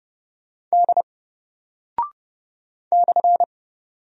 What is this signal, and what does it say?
Morse code, heard as a keyed tone, about 30 wpm, 700 Hz. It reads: D /
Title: Single Character (Letter:Number:Punctuation:Prosign) 30wpm